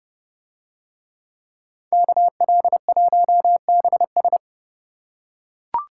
Morse code, heard as a keyed tone, about 29 wpm, 700 Hz. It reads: KL1BS